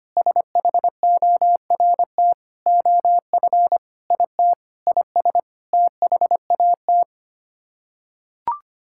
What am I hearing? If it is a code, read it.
Morse code, heard as a keyed tone, about 25 wpm, 700 Hz. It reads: SHORT OF IT IS THAT
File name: Sets of 9 Words - 25wpm